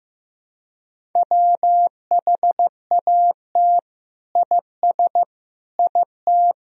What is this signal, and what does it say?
Morse code, heard as a keyed tone, about 15 wpm, 700 Hz. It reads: WHAT IS IT
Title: Sentences from Top 100 Words and Abbreviations 15wpm